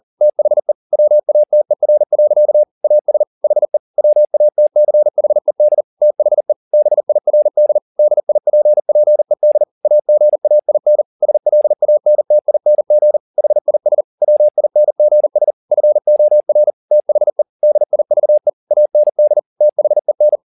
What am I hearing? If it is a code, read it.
Morse code, heard as a keyed tone, about 40 wpm, 600 Hz. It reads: THE WATER. AS HE WATCHED THE BIRD DIPPED AGAIN SLANTING HIS WINGS FOR THE DIVE AND THEN